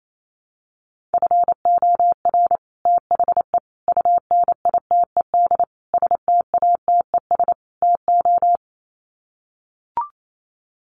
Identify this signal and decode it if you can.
Morse code, heard as a keyed tone, about 28 wpm, 700 Hz. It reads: FOR THE UNITED STATES TO
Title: Sets of 5 Words Encore 28wpm